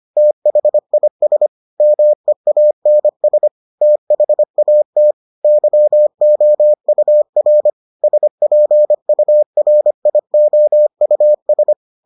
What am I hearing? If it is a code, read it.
Morse code, heard as a keyed tone, about 25 wpm, 600 Hz. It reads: THIS MEANS THAT YOUR SPURIOUS